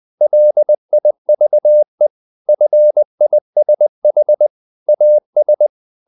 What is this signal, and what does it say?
Morse code, heard as a keyed tone, about 20 wpm, 600 Hz. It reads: LIVE FISH AS
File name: the old man and the sea part 2 20wpm